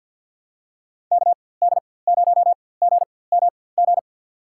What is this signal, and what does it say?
Morse code, heard as a keyed tone, about 47 wpm, spaced as for 23 wpm, 700 Hz. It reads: KD0GMG
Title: USA Call Signs - 6 Characters - 4x Character Spacing - 50wpm